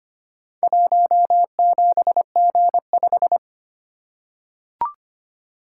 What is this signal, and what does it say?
Morse code, heard as a keyed tone, about 25 wpm, 700 Hz. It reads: 17G5
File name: practice / 25wpm / Four Letter-Number Combinations - 25wpm